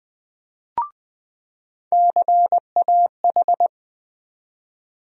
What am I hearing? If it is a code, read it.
Morse code, heard as a keyed tone, about 21 wpm, 700 Hz. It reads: CAH
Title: USA Call Sign Postfixes - 3 Letters 20wpm